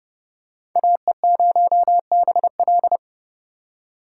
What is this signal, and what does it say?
Morse code, heard as a keyed tone, about 30 wpm, 700 Hz. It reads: AE0BL